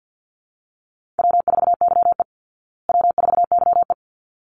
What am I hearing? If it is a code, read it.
Morse code, heard as a keyed tone, about 48 wpm, 700 Hz. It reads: W4YE W4YE
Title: Easy Call Signs Repeated 2x - 50wpm